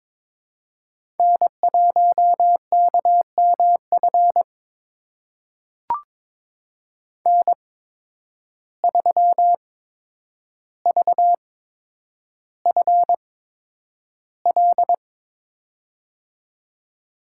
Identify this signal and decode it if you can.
Morse code, heard as a keyed tone, about 22 wpm, spaced as for 9 wpm, 700 Hz. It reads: N1KMF N3VFL
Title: USA Call Signs - 5 Characters - 8x Character Spacing - 22wpm